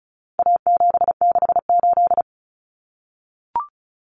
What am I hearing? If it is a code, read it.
Morse code, heard as a keyed tone, about 34 wpm, 700 Hz. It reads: A768
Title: Common Cut Numbers - 4-Digit 35wpm